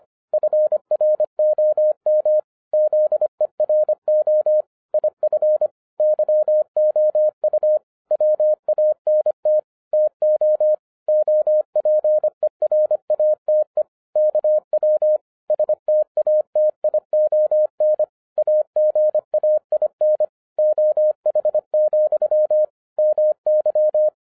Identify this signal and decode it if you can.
Morse code, heard as a keyed tone, about 24 wpm, 610 Hz. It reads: FROM ZERO IF YOU WANT TO OPERATE KW STATION AGAIN OH, MY